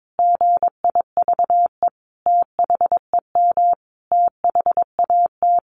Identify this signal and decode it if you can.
Morse code, heard as a keyed tone, about 22 wpm, 700 Hz. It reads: GIVE THEM THAT
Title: Sentences from Top 1100 Words 22wpm